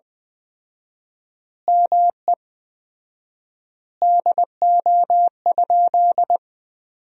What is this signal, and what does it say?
Morse code, heard as a keyed tone, about 20 wpm, 700 Hz. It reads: ME DO?
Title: 4x Word-Spacing - Sentences from Top 200 Words - 20wpm